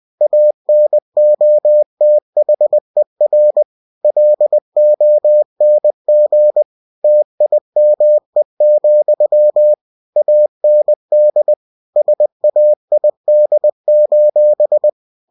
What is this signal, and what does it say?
Morse code, heard as a keyed tone, about 20 wpm, 600 Hz. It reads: ANOTHER LONG TIME, AND SAID:
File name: cw-books / winnie-the-pooh / winnie-20wpm-page-0002